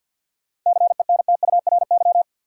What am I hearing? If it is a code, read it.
Morse code, heard as a keyed tone, about 49 wpm, 700 Hz. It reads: CENTURY
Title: Sets of 5 Words - New Word Drill - 50wpm